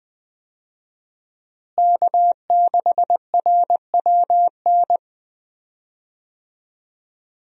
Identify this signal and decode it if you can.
Morse code, heard as a keyed tone, about 20 wpm, 700 Hz. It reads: K6RWN
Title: USA Call Signs - 5 Characters - Repeated 2x - 20wpm